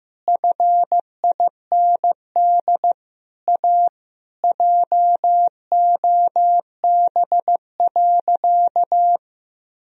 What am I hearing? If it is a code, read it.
Morse code, heard as a keyed tone, about 15 wpm, 700 Hz. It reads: FIND A JOB.
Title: Sentences from Top 1000 Words - Review Part 3 - 15wpm